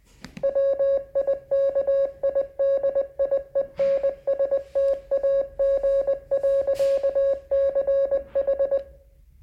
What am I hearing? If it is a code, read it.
Morse code, heard as a keyed tone, about 20 wpm, 550 Hz. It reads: WIKIDIENSTAG.CH